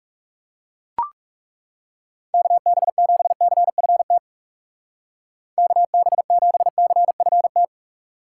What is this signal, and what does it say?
Morse code, heard as a keyed tone, about 38 wpm, 700 Hz. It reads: KB7CFT KB7CFT